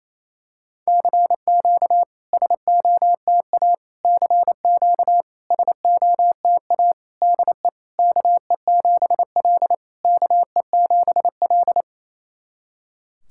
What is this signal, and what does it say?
Morse code, heard as a keyed tone, about 28 wpm, 690 Hz. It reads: CQ SOTA CQ SOTA DE KE7L KE7L
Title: SOTA QSO Practice 28wpm